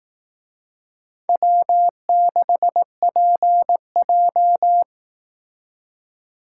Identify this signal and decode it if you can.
Morse code, heard as a keyed tone, about 18 wpm, 700 Hz. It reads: W6PJ
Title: Speed Racing - USA 1x2 Call Signs 15wpm